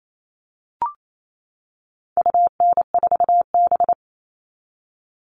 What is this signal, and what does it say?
Morse code, heard as a keyed tone, about 28 wpm, 700 Hz. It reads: UN4B
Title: All Cut Numbers - 4-Digit 28wpm